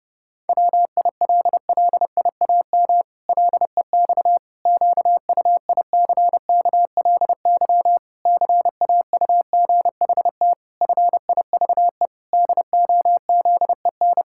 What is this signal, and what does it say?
Morse code, heard as a keyed tone, about 32 wpm, 700 Hz. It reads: WILLIAM LEX QUICKLY CAUGHT FIVE DOZEN
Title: Pangram Challenge v2 - 30wpm